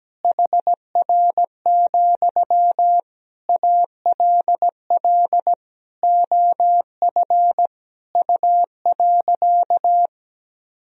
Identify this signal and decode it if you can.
Morse code, heard as a keyed tone, about 17 wpm, 700 Hz. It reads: HR, ALL OF U.
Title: Sentences from Top 500 Words and Abbreviations 17wpm